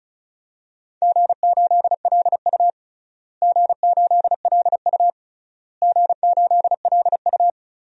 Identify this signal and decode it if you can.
Morse code, heard as a keyed tone, about 35 wpm, 700 Hz. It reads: G8LU G8LU G8LU
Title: DXCC-100 Call Signs - 4 Characters - Repeated 3x - 35wpm